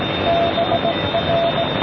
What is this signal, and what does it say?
Morse code, heard as a keyed tone, about 18 wpm, 670 Hz, with noise. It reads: BR